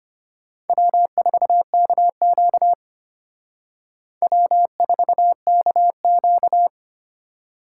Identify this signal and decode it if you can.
Morse code, heard as a keyed tone, about 32 wpm, 700 Hz. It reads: W4KQ W4KQ